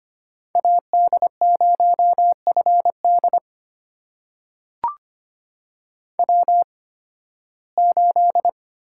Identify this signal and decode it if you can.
Morse code, heard as a keyed tone, about 25 wpm, spaced as for 16 wpm, 700 Hz. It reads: AD0FD W8
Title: USA Call Signs - 5 Characters - 8x Character Spacing - 25wpm